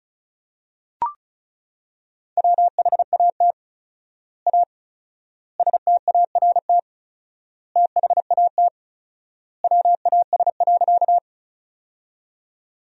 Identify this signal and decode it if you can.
Morse code, heard as a keyed tone, about 34 wpm, 700 Hz. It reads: WHAT A START THAT WAS.
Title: 4x Word-Spacing - Sentences from Top 300 Words - 35wpm